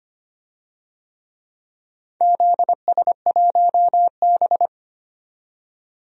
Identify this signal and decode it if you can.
Morse code, heard as a keyed tone, about 25 wpm, 700 Hz. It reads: ZS1B
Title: DXCC-100 Call Signs - 4 Characters - 25wpm